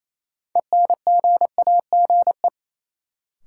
Morse code, heard as a keyed tone, about 28 wpm, 700 Hz. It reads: ENGAGE
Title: Sentences from Top 1500 Words - New Word Drill Alt - 28wpm